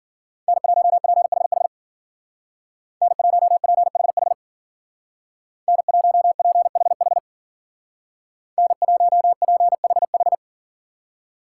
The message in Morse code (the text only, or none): N1PHH N1PHH N1PHH N1PHH